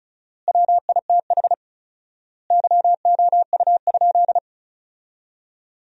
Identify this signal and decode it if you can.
Morse code, heard as a keyed tone, about 34 wpm, 700 Hz. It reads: WITH YOU?
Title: 4x Word-Spacing - Sentences from Top 100 Words - 35wpm